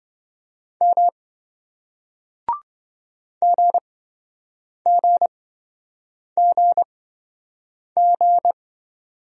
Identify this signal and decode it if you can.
Morse code, heard as a keyed tone, about 28 wpm, spaced as for 7 wpm, 700 Hz. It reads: M GGGG